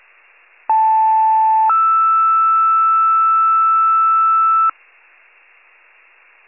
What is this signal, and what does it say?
Two-tone page, heard as a one-second tone, then a three-second tone, about 880 then 1320 Hz. It is heard from 0.7 to 4.7 s.